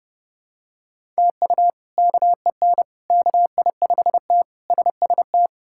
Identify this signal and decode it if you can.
Morse code, heard as a keyed tone, about 30 wpm, 700 Hz. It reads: TU KEN KI5T SST
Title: K1USN SST QSO Practice - 30wpm